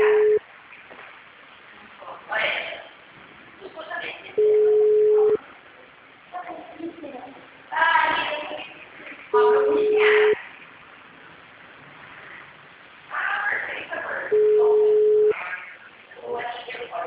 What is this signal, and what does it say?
Ringback tone (Europe, 425 Hz), with speech.
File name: out- -2017- - - .71056